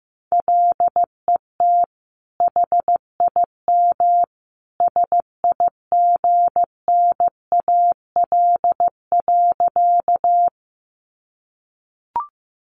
Morse code, heard as a keyed tone, about 15 wpm, 700 Hz. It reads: LET HIM SIGNAL.
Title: Sentences from Top 1400 Words 15wpm